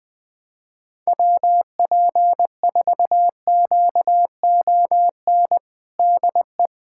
Morse code, heard as a keyed tone, about 20 wpm, 690 Hz. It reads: WP4QON DE
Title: USA Callsign DE Callsign 20wpm